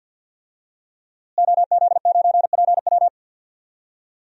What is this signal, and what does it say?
Morse code, heard as a keyed tone, about 53 wpm, 700 Hz. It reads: OZ9PW
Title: DXCC-100 Call Signs - 5 Characters - Repeated 2x - 50wpm